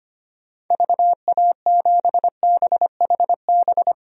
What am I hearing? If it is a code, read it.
Morse code, heard as a keyed tone, about 25 wpm, 700 Hz. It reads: VA7BHB
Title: DXCC-100 Call Signs - 6 Characters - Repeated 3x - 25wpm